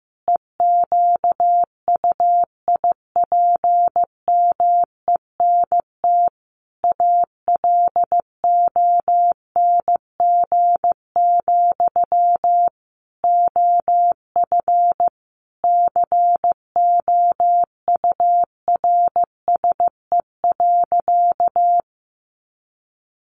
Morse code, heard as a keyed tone, about 15 wpm, 700 Hz. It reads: EQUIPMENT ALONG, OF COURSE.
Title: Sentences from Top 700 Words 15wpm